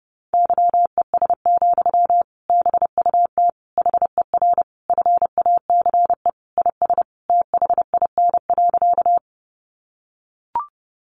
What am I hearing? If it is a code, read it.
Morse code, heard as a keyed tone, about 30 wpm, 700 Hz. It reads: YES, BUT HER FACE IS THIN.
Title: Sentences from Top 1200 Words 30wpm